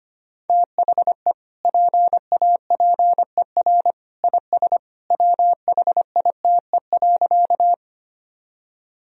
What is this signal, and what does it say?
Morse code, heard as a keyed tone, about 25 wpm, 700 Hz. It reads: THE PAPER IS WHITE.
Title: Sentences from Top 1000 Words - Review Part 1 - 25wpm